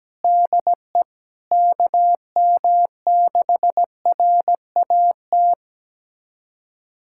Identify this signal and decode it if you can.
Morse code, heard as a keyed tone, about 17 wpm, 700 Hz. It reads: DE KM6RAT
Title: USA Callsign DE Callsign - 6 Characters - 17wpm